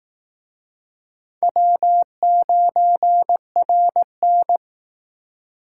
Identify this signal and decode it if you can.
Morse code, heard as a keyed tone, about 19 wpm, 700 Hz. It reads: W9RN
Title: Speed Racing - USA Call Signs 15wpm